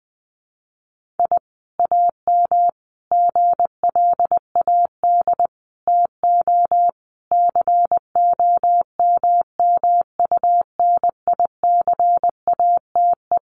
Morse code, heard as a keyed tone, about 20 wpm, 700 Hz. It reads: I AM GLAD TO COMMUNICATE
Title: Sentences from Top 1300 Words 20wpm